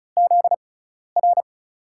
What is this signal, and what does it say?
Morse code, heard as a keyed tone, about 34 wpm, spaced as for 12 wpm, 700 Hz. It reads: ZR
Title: USA Call Signs - 6x Character Spacing - 35wpm